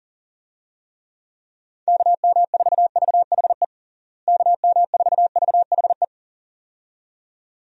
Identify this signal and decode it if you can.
Morse code, heard as a keyed tone, about 40 wpm, 700 Hz. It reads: KM4VHE KM4VHE
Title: USA Call Signs - 6 Characters - Repeated 2x - 40wpm